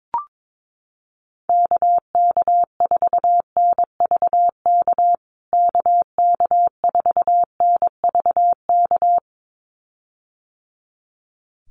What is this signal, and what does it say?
Morse code, heard as a keyed tone, about 22 wpm, 700 Hz. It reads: KK4NVK KK4NVK
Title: USA Call Signs Repeated 2x 22wpm